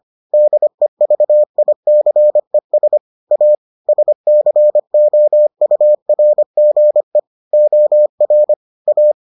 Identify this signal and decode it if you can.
Morse code, heard as a keyed tone, about 25 wpm, 600 Hz. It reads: DEVICES A SCOURGE OR A